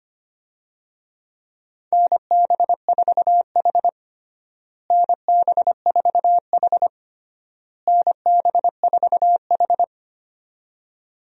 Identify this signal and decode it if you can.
Morse code, heard as a keyed tone, about 25 wpm, 700 Hz. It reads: NB4H NB4H NB4H
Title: USA Call Signs - 4 Characters - Repeated 3x - 25wpm